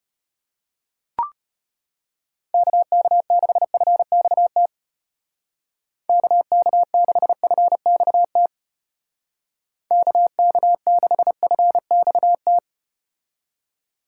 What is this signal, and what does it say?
Morse code, heard as a keyed tone, about 35 wpm, 700 Hz. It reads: KK6FXT KK6FXT KK6FXT